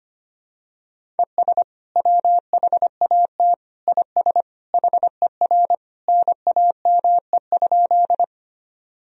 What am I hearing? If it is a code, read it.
Morse code, heard as a keyed tone, about 25 wpm, 700 Hz. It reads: ES WHAT IS HER NAME?
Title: Sentences from Top 200 Words and Abbreviations 25wpm